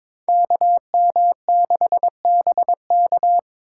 Morse code, heard as a keyed tone, about 21 wpm, 700 Hz. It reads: KM6BK
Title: USA Call Signs - 5 Characters - 3x Character Spacing - 22wpm